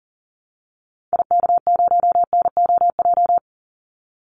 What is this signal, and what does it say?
Morse code, heard as a keyed tone, about 40 wpm, 700 Hz. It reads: IK0NOJ
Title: Call Signs Repeated 2x - 40wpm